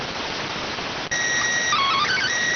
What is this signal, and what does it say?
SSTV (slow-scan television), PD 290.